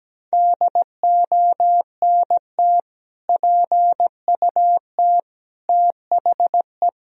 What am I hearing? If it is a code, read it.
Morse code, heard as a keyed tone, about 17 wpm, 700 Hz. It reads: DONT PUT THE
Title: Sentences from Top 1000 Words - Review Part 2 - 17wpm